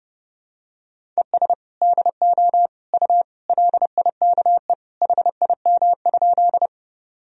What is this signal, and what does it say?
Morse code, heard as a keyed tone, about 30 wpm, 700 Hz. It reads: ES DO U LIKE HIM?
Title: Sentences from Top 100 Words and Abbreviations 30wpm